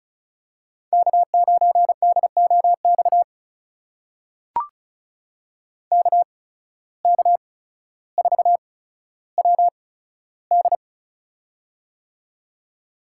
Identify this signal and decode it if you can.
Morse code, heard as a keyed tone, about 36 wpm, spaced as for 15 wpm, 700 Hz. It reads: K9DOX KK4WD
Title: USA Call Signs - 5 Characters - 8x Character Spacing - 35wpm